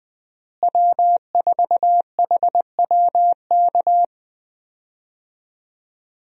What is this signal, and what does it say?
Morse code, heard as a keyed tone, about 21 wpm, 700 Hz. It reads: W4HWK
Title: USA Call Signs - 5 Characters - Repeated 2x - 20wpm